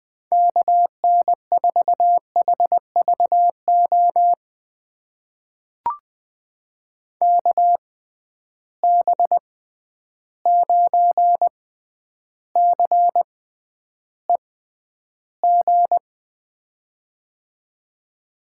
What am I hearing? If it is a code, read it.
Morse code, heard as a keyed tone, about 20 wpm, spaced as for 10 wpm, 700 Hz. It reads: KN4HVO KB9CEG
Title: USA Call Signs - 6 Characters - 6x Character Spacing - 20wpm